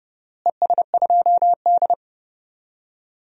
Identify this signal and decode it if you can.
Morse code, heard as a keyed tone, about 30 wpm, 700 Hz. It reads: ES2D
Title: DXCC-100 Call Signs - 4 Characters - Repeated 3x - 30wpm